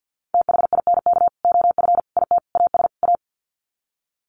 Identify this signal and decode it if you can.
Morse code, heard as a keyed tone, about 50 wpm, 700 Hz. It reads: THINK OF IT AS A
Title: Sets of 5 Words Encore2 50wpm